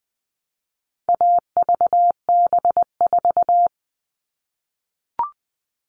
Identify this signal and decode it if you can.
Morse code, heard as a keyed tone, about 20 wpm, 700 Hz. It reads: AVB4
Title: All Cut Numbers - 4-Digit 20wpm